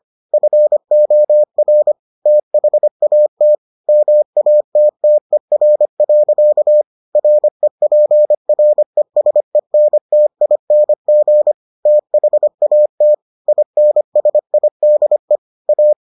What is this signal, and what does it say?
Morse code, heard as a keyed tone, about 25 wpm, 600 Hz. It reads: FOR THAT MATTER. REPRESENTING THAT INSIDE A